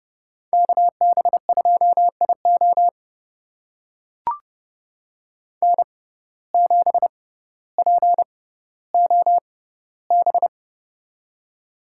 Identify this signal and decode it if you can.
Morse code, heard as a keyed tone, about 30 wpm, spaced as for 15 wpm, 690 Hz. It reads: KB2IO N7POB
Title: USA Call Signs - 5 Characters - 6x Character Spacing - 30wpm